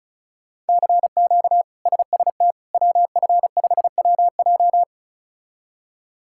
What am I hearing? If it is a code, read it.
Morse code, heard as a keyed tone, about 36 wpm, 700 Hz. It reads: CQ SST WF5WJ
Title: K1USN SST QSO Practice - 35wpm